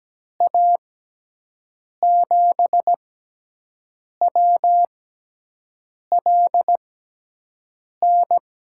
Morse code, heard as a keyed tone, about 17 wpm, spaced as for 6 wpm, 700 Hz. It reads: A7WLN
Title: USA Call Signs - 6x Character Spacing - 17wpm